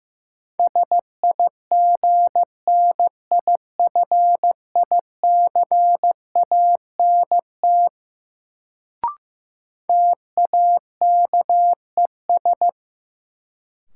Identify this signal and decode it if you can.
Morse code, heard as a keyed tone, about 16 wpm, 700 Hz. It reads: SIGNIFICANT TAKES
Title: Sentences from Top 1000 Words - Review Part 6 - New Word Drill - 15wpm